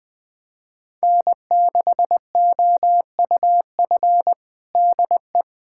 Morse code, heard as a keyed tone, about 21 wpm, 700 Hz. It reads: N6OUF DE
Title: USA Callsign DE Callsign - 5 Characters - 20wpm